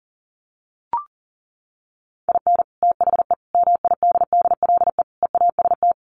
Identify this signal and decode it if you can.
Morse code, heard as a keyed tone, about 40 wpm, 700 Hz. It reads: IN THE MIDDLE EAST
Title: Sets of 4 Words 40wpm